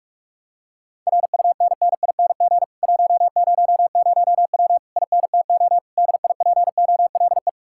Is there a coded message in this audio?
Yes: Morse code from 1.1 to 7.5 s, heard as a keyed tone, about 40 wpm, 700 Hz. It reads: RUNNING 100W INTO DIPOLE